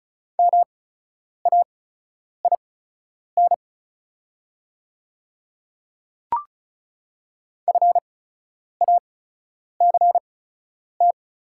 Morse code, heard as a keyed tone, about 36 wpm, spaced as for 10 wpm, 690 Hz. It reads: MAIN FACT
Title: 8x Character Spacing - Top 500 Words - 35wpm